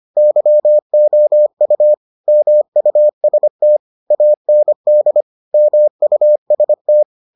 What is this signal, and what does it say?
Morse code, heard as a keyed tone, about 25 wpm, 600 Hz. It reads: YOU MUST AND MUST